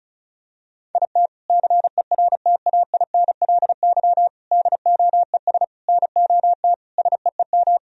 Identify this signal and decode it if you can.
Morse code, heard as a keyed tone, about 34 wpm, 700 Hz. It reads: IT CERTAINLY DOES NOT SEEM
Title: Sentences from Top 1000 Words - Review Part 1 - 35wpm